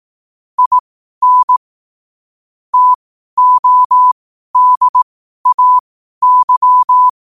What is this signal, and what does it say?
Morse code, heard as a keyed tone, about 18 wpm, spaced as for 12 wpm, 1000 Hz. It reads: IN TODAY